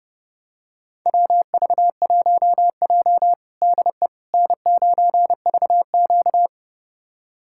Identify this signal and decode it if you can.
Morse code, heard as a keyed tone, about 30 wpm, 700 Hz. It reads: WV1J DE N9VQ
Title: USA Callsign DE Callsign - 4 Characters - 30wpm